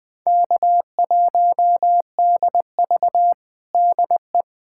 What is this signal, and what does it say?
Morse code, heard as a keyed tone, about 21 wpm, 700 Hz. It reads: K1DV DE
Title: USA Callsign DE Callsign - 4 Characters - 20wpm